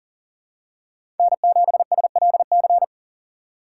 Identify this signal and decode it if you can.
Morse code, heard as a keyed tone, about 40 wpm, 700 Hz. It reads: N7SLC
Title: USA Call Signs - 5 Characters - 2x Character Spacing - 40wpm